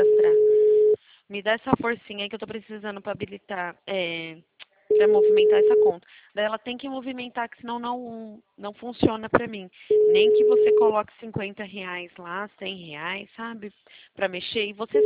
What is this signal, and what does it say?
Ringback tone (Europe, 425 Hz), with speech.